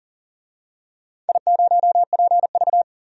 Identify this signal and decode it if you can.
Morse code, heard as a keyed tone, about 40 wpm, 700 Hz. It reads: I0PV
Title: DXCC-100 Call Signs - 4 Characters - 40wpm